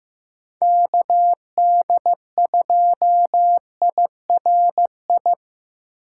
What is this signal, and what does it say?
Morse code, heard as a keyed tone, about 18 wpm, 700 Hz. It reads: KD2IRI
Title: Speed Racing - USA 2x3 Call Signs 15wpm